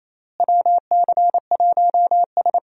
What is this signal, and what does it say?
Morse code, heard as a keyed tone, about 28 wpm, 700 Hz. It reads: WC1S
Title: USA Call Signs - 4 Characters - Repeated 3x - 28wpm